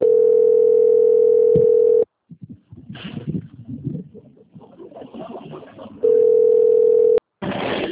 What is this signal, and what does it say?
Ringback tone (North America), 440 + 480 Hz.